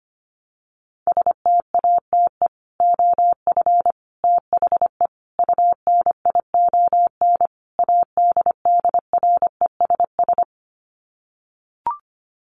Morse code, heard as a keyed tone, about 25 wpm, 700 Hz. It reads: STATE OF THE UNION ADDRESS
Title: Sets of 5 Words Encore 25wpm